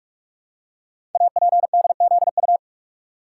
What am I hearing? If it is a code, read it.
Morse code, heard as a keyed tone, about 44 wpm, 700 Hz. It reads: APDZU
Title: Code Groups - 5 Letters - 45wpm